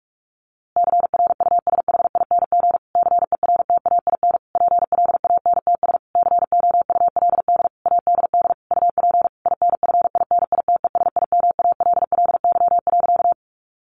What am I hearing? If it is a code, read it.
Morse code, heard as a keyed tone, about 44 wpm, 700 Hz. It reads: CRUSHING CERTAIN PLANTS COULD ADD UP INFINITESIMALLY.
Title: A Sound Of Thunder 45wpm